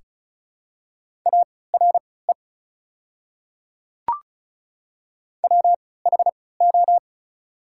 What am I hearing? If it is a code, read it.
Morse code, heard as a keyed tone, about 36 wpm, spaced as for 20 wpm, 700 Hz. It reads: ARE WHO
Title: 3x Character Spacing - Top 200 Words - 35wpm